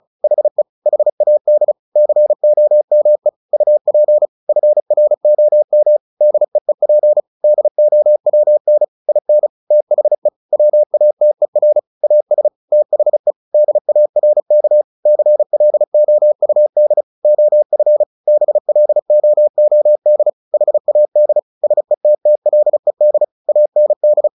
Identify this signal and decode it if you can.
Morse code, heard as a keyed tone, about 34 wpm, 600 Hz. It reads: HE HAD COME UP FROM DEEP DOWN IN THE WATER AS THE DARK CLOUD OF BLOOD HAD SETTLED AND
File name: the old man and the sea 2 35wpm